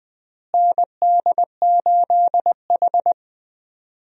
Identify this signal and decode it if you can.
Morse code, heard as a keyed tone, about 20 wpm, 700 Hz. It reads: ND8H
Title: USA Call Signs - 4 Characters - Repeated 3x - 20wpm